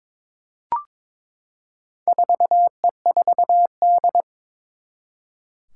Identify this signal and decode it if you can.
Morse code, heard as a keyed tone, about 22 wpm, 700 Hz. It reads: 4E4D
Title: Four Challenging Letter-Number Combinations 22wpm